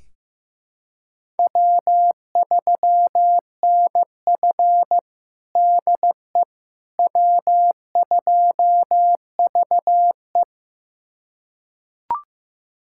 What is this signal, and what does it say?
Morse code, heard as a keyed tone, about 15 wpm, 700 Hz. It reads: W3NF DE W2VE
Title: USA Callsign DE Callsign - 4 Characters - 15wpm